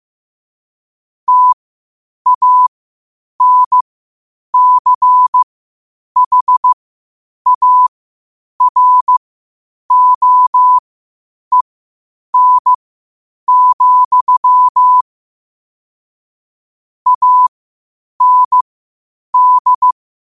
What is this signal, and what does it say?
Morse code, heard as a keyed tone, about 15 wpm, spaced as for 8 wpm, 1000 Hz. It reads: TANCHAROEN, AND
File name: qotdfiles / 8wpm